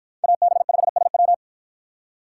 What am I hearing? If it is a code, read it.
Morse code, heard as a keyed tone, about 53 wpm, 700 Hz. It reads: AB5SW